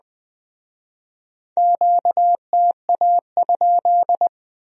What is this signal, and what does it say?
Morse code, heard as a keyed tone, about 20 wpm, 700 Hz. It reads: QTA?
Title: Speed-Racing Q-Codes - 17wpm